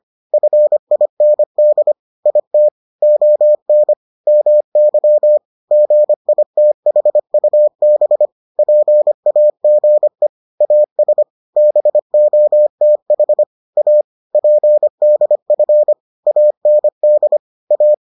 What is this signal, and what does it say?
Morse code, heard as a keyed tone, about 25 wpm, 600 Hz. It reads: FIND IT ON MY GITHUB PAGE AS BOTH A PDF AND A